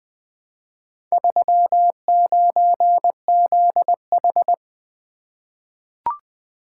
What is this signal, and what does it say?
Morse code, heard as a keyed tone, about 20 wpm, 700 Hz. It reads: 39ZH
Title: Four Letter-Number Combinations - 20wpm